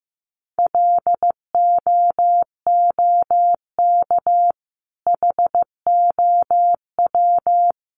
Morse code, heard as a keyed tone, about 15 wpm, 700 Hz. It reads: LOOK HOW
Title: Sentences from Top 1400 Words 15wpm